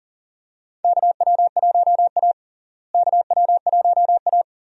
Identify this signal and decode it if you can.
Morse code, heard as a keyed tone, about 40 wpm, 700 Hz. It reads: KW1A KW1A
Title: USA Call Signs - 4 Characters - Repeated 2x - 40wpm